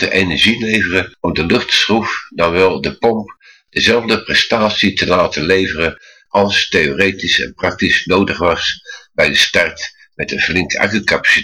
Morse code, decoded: none